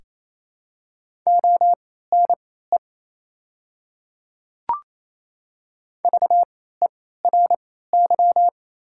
Morse code, heard as a keyed tone, about 28 wpm, spaced as for 16 wpm, 700 Hz. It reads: ONE VERY
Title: 3x Character Spacing - Top 200 Words - 28wpm